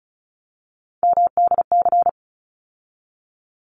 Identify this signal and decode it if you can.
Morse code, heard as a keyed tone, about 34 wpm, 700 Hz. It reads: MDC
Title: ARRL Sections 35wpm